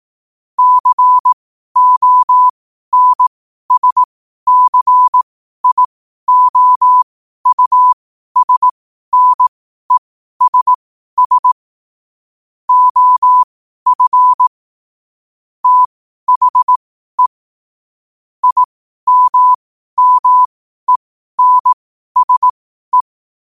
Morse code, heard as a keyed tone, about 18 wpm, spaced as for 12 wpm, 1000 Hz. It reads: CONSCIOUSNESS OF THE IMMENSE
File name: qotdfiles / 12wpm